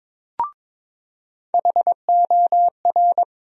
Morse code, heard as a keyed tone, about 22 wpm, 700 Hz. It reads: HOR
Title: USA Call Sign Postfixes - 3 Letters 22wpm